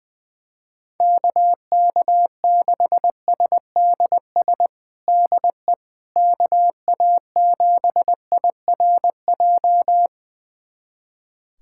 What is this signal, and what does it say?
Morse code, heard as a keyed tone, about 20 wpm, 700 Hz. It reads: KK6SDS DE KA7IRJ
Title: USA Callsign DE Callsign - 6 Characters - 20wpm